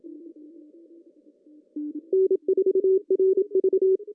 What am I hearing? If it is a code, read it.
Morse code, heard as a keyed tone, about 26 wpm, 380 Hz. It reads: N4RV
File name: CW / cqww-cw-k5tr